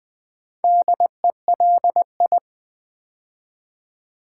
Morse code, heard as a keyed tone, about 20 wpm, 700 Hz. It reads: DELI
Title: Build A Word - 2 Letters At A Time - 8-Letter Words - 20wpm